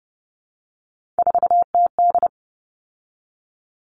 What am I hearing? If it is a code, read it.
Morse code, heard as a keyed tone, about 30 wpm, 700 Hz. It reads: 4TD
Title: All Cut Numbers - 3-Digit 30wpm